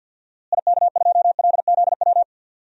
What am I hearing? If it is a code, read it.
Morse code, heard as a keyed tone, about 52 wpm, 700 Hz. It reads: IK2LZW
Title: DXCC-100 Call Signs - 6 Characters - Repeated 2x - 50wpm